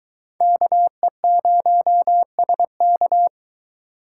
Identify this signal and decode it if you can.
Morse code, heard as a keyed tone, about 22 wpm, 700 Hz. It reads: KE0SK